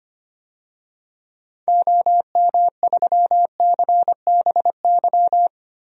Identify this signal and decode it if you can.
Morse code, heard as a keyed tone, about 25 wpm, 700 Hz. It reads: OM3CBY
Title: DXCC-100 Call Signs - 6 Characters - 25wpm